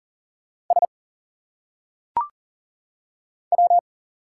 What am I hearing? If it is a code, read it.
Morse code, heard as a keyed tone, about 39 wpm, 700 Hz. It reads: S W
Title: Single Letters - Phonetic Pronunciation 40wpm